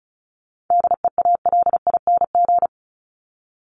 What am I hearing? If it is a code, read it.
Morse code, heard as a keyed tone, about 34 wpm, 700 Hz. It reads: DEALING
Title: ING Suffix - Top 100 Words with Base Words 35wpm